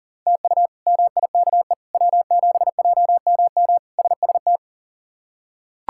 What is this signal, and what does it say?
Morse code, heard as a keyed tone, about 40 wpm, 700 Hz. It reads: TU MIKE W7JMM SST
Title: K1USN SST QSO Practice - 40wpm